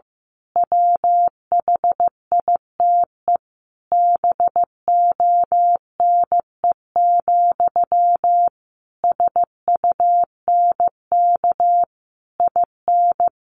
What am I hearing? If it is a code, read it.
Morse code, heard as a keyed tone, about 15 wpm, 700 Hz. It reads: WHITE BONE, SUNK IN
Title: A Sound Of Thunder 15wpm